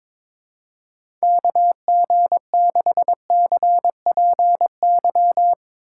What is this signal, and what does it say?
Morse code, heard as a keyed tone, about 23 wpm, 700 Hz. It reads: KG6CPY
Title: USA Call Signs - 6 Characters - 2x Character Spacing - 22wpm